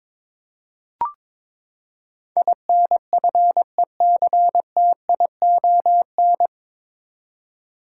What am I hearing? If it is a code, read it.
Morse code, heard as a keyed tone, about 21 wpm, 700 Hz. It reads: INFECTION
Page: Morse Code Ninja practice 22wpm